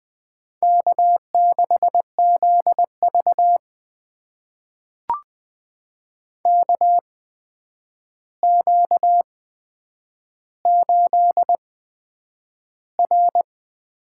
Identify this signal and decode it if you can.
Morse code, heard as a keyed tone, about 21 wpm, spaced as for 9 wpm, 700 Hz. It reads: K6ZV KQ8R
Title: USA Call Signs - 4 Characters - 8x Character Spacing - 20wpm